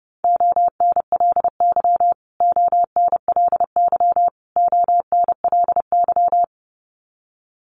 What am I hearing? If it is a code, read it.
Morse code, heard as a keyed tone, about 30 wpm, 700 Hz. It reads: ONLY ONLY ONLY
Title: Top 500 Words Repeated 30wpm